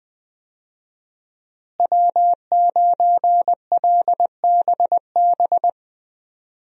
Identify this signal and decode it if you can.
Morse code, heard as a keyed tone, about 20 wpm, 700 Hz. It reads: W9LBB
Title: USA Call Signs - 5 Characters - Repeated 3x - 20wpm